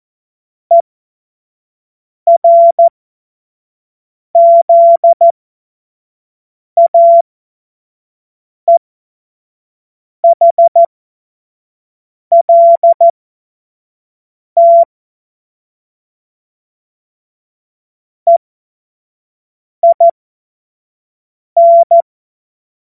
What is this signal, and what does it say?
Morse code, heard as a keyed tone, about 14 wpm, spaced as for 5 wpm, 680 Hz. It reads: ERZAEHLT EIN
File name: Witz-05-August-2025-25-F70